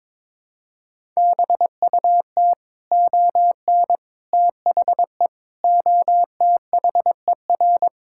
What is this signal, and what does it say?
Morse code, heard as a keyed tone, about 22 wpm, 700 Hz. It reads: BUT ON THE OTHER
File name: practice / 22wpm / Build A Phrase - 5-Word Phrases - 22wpm